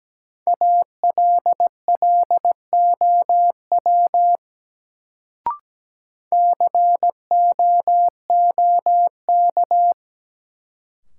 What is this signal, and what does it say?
Morse code, heard as a keyed tone, about 17 wpm, 700 Hz. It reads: ALLOW COOK
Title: Sentences from Top 700 Words - New Word Drill - 17wpm